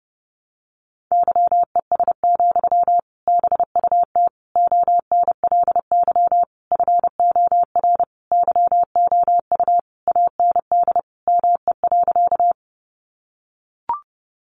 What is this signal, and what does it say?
Morse code, heard as a keyed tone, about 30 wpm, 700 Hz. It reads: YES, BUT ONLY FOR YOU AND ME.
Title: Sentences from Top 500 Words 30wpm